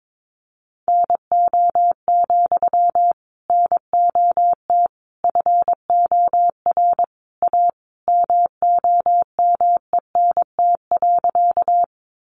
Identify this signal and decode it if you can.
Morse code, heard as a keyed tone, about 22 wpm, 700 Hz. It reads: NO, NOT FOR A MOMENT.
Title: Sentences from Top 1000 Words 22wpm